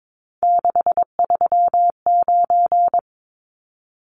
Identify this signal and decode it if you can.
Morse code, heard as a keyed tone, about 22 wpm, 700 Hz. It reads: 639
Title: Numbers - 3-Digit 22wpm